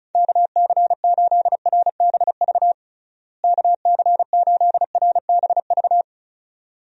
Morse code, heard as a keyed tone, about 36 wpm, 700 Hz. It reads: KC8RBV KC8RBV
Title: USA Call Signs - 6 Characters - Repeated 2x - 35wpm